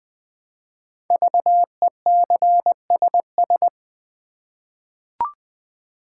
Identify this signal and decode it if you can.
Morse code, heard as a keyed tone, about 20 wpm, 700 Hz. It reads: VECSS
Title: Code Groups - 5 Letters - 20wpm